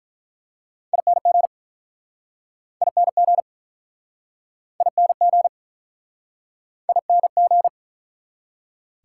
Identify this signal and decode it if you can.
Morse code, heard as a keyed tone, about 48 wpm, 700 Hz. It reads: ING ING ING ING